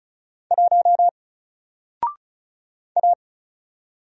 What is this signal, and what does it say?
Morse code, heard as a keyed tone, about 36 wpm, 700 Hz. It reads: 1 A
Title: Single Letter-Number Alt 35wpm